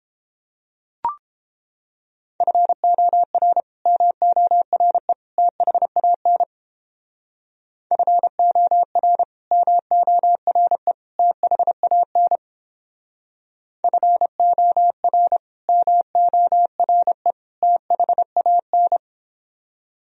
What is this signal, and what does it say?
Morse code, heard as a keyed tone, about 29 wpm, 700 Hz. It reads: FOR MORE THAN FOR MORE THAN FOR MORE THAN